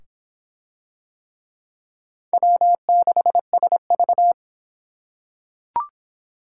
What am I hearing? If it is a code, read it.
Morse code, heard as a keyed tone, about 23 wpm, 700 Hz. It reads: W6SV